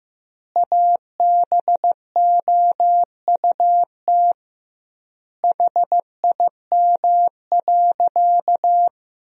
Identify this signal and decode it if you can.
Morse code, heard as a keyed tone, about 15 wpm, 700 Hz. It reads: ABOUT HIM.
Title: 2x Word-Spacing - Sentences from Top 200 Words - 15wpm